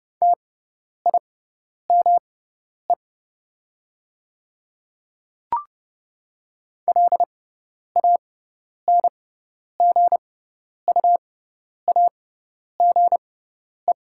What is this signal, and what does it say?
Morse code, heard as a keyed tone, about 30 wpm, spaced as for 10 wpm, 690 Hz. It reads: TIME LANGUAGE
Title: 6x Character Spacing - Top 500 Words - 30wpm